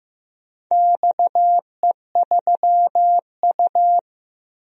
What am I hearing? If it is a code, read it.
Morse code, heard as a keyed tone, about 15 wpm, 700 Hz. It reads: XE3U
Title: DXCC-100 Call Signs - 4 Characters - 15wpm